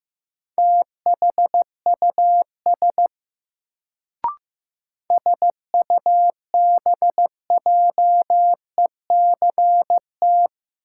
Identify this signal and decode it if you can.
Morse code, heard as a keyed tone, about 15 wpm, 700 Hz. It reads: THUS SUBJECT
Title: Sentences from Top 800 Words - New Word Drill - 15wpm